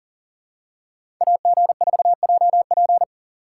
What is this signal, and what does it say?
Morse code, heard as a keyed tone, about 39 wpm, 700 Hz. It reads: AG4JP
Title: USA Call Signs - 5 Characters - 2x Character Spacing - 40wpm